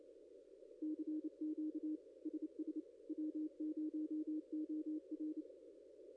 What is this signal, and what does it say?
Morse code, heard as a keyed tone, about 29 wpm, 310 Hz. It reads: CQ SS W0OR
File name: contests / ss-cw-k5tr-op-n5rz -0900